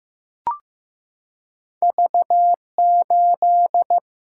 Morse code, heard as a keyed tone, about 15 wpm, 700 Hz. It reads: V 8
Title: Single-Letter Single-Number 15wpm